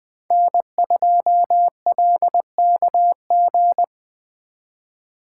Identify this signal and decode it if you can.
Morse code, heard as a keyed tone, about 25 wpm, 700 Hz. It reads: N2LKG
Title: Speed Racing - USA 1x3 Call Signs 20wpm